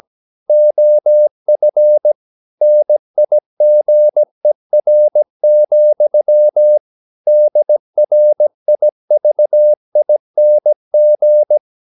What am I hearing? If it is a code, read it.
Morse code, heard as a keyed tone, about 17 wpm, 600 Hz. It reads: OF NIGER, DRIVING
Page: Romanen Never af Ken Follet på engelsk.